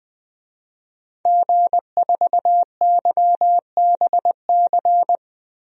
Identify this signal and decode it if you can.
Morse code, heard as a keyed tone, about 20 wpm, 700 Hz. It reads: G4YBC
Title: DXCC-100 Call Signs - 5 Characters - Repeated 2x - 20wpm